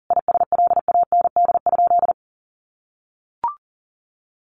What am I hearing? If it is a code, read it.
Morse code, heard as a keyed tone, about 40 wpm, 700 Hz. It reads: ISLAND?
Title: Sentences from Top 1100 Words 40wpm